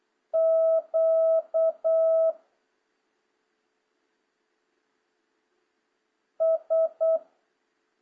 Morse code, heard as a keyed tone, about 8 wpm, 650 Hz. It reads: Q S